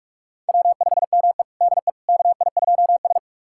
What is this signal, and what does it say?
Morse code, heard as a keyed tone, about 46 wpm, 700 Hz. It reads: W5ME DE KI2S
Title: USA Callsign DE Callsign - 4 Characters - 45wpm